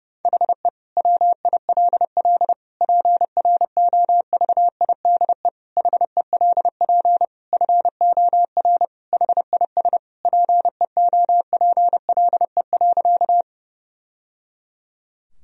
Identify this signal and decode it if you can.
Morse code, heard as a keyed tone, about 32 wpm, 700 Hz. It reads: HE WILL PROVIDE HELP FOR HIS PEOPLE.
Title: Sentences from Top 1000 Words - Review Part 3 - 30wpm